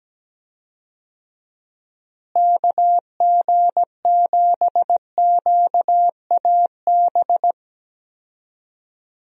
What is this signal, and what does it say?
Morse code, heard as a keyed tone, about 17 wpm, 700 Hz. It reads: KG7QAB
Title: USA Call Signs - 6 Characters - Repeated 3x - 17wpm